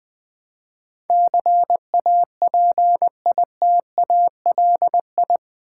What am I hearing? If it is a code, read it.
Morse code, heard as a keyed tone, about 20 wpm, 700 Hz. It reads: CAPITALI
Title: Build A Word - 4 Letters At A Time - 10-Letter Words - 20wpm